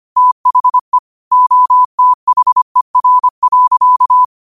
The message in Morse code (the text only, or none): THE OTHER.